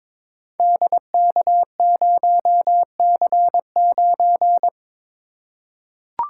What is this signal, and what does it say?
Morse code, heard as a keyed tone, about 22 wpm, 700 Hz. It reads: DK0C9
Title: Code Groups - 5 Letter:Numbers - 22wpm